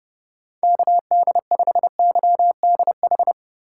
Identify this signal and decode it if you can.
Morse code, heard as a keyed tone, about 30 wpm, 700 Hz. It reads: KD5YDH
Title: USA Call Signs - 6 Characters - 4x Character Spacing - 30wpm